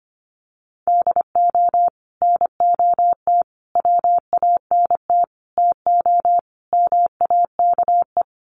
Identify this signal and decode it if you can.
Morse code, heard as a keyed tone, about 25 wpm, 700 Hz. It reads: DO NOT WANT TO MAKE
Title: Sets of 5 Words Encore 25wpm